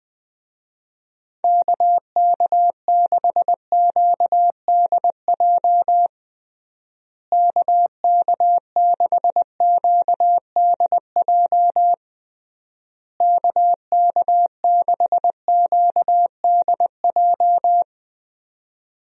Morse code, heard as a keyed tone, about 20 wpm, 700 Hz. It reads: KK6QDJ KK6QDJ KK6QDJ
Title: USA Call Signs - 6 Characters - Repeated 3x - 20wpm